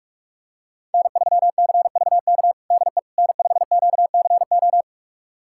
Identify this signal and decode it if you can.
Morse code, heard as a keyed tone, about 41 wpm, 700 Hz. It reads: N3CVK DE N5QCO
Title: USA Callsign DE Callsign - 5 Characters - 45wpm